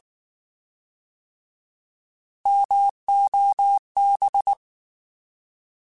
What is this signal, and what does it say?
Morse code, heard as a keyed tone, about 19 wpm, 800 Hz. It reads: MOB
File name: Alphabetical / Alphabetical mp3s / Ms @18